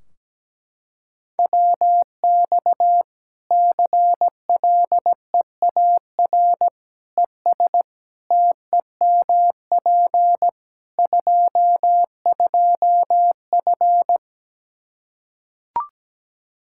Morse code, heard as a keyed tone, about 17 wpm, 700 Hz. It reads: WX CLEAR ES TEMP 22F
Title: Practice QSO - Exchange 2 - Info - 17wpm